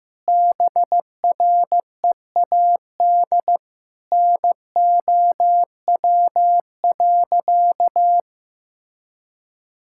Morse code, heard as a keyed tone, about 15 wpm, 700 Hz. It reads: BREAD NOW.
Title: Sentences from Top 1600 Words 15wpm